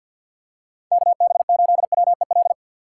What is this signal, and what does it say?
Morse code, heard as a keyed tone, about 48 wpm, 700 Hz. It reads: KB8WEL
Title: USA Call Signs - 6 Characters - 6x Character Spacing - 50wpm